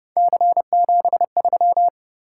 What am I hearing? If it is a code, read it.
Morse code, heard as a keyed tone, about 31 wpm, 700 Hz. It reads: C73